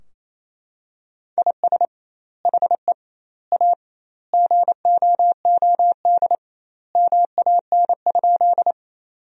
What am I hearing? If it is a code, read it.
Morse code, heard as a keyed tone, about 28 wpm, 700 Hz. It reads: IS HE A GOOD MAN?
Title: 2x Word-Spacing - Sentences from Top 200 Words - 28wpm